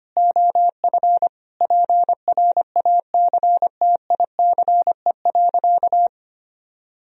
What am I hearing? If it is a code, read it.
Morse code, heard as a keyed tone, about 25 wpm, 700 Hz. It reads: OF PRACTICE.
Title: Sentences from Top 1000 Words - Review Part 4 - 25wpm